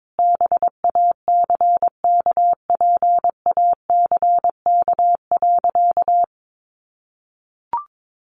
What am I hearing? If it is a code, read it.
Morse code, heard as a keyed tone, about 22 wpm, 700 Hz. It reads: BACKPACK.
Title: Additional Easy Phrases 22wpm